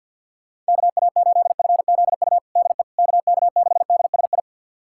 Morse code, heard as a keyed tone, about 51 wpm, 690 Hz. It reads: KA8FZU DE KK6DSI
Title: USA Callsign DE Callsign - 6 Characters - 50wpm